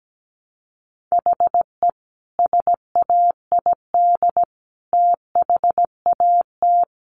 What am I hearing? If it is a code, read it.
Morse code, heard as a keyed tone, about 17 wpm, 700 Hz. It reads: HE SAID THAT
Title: Sentences from Top 1200 Words 17wpm